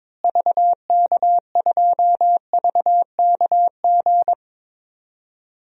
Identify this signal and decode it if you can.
Morse code, heard as a keyed tone, about 22 wpm, 700 Hz. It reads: VK2VKG
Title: DXCC-100 Call Signs - 6 Characters - Repeated 2x - 22wpm